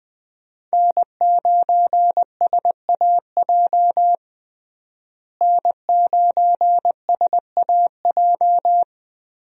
Morse code, heard as a keyed tone, about 21 wpm, 700 Hz. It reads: N9SAJ N9SAJ
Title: USA Call Signs - 5 Characters - Repeated 3x - 20wpm